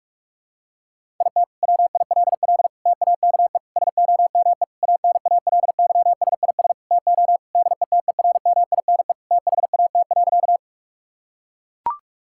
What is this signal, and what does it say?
Morse code, heard as a keyed tone, about 42 wpm, 700 Hz. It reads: IT WILL TAKE SOME ANALYSIS TO DETERMINE THAT.
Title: Sentences from Top 1000 Words - Review Part 5 - 45wpm